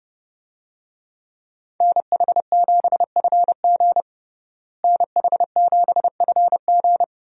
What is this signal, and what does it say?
Morse code, heard as a keyed tone, about 31 wpm, 700 Hz. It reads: NH7FG NH7FG
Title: USA Call Signs - 5 Characters - Repeated 2x - 30wpm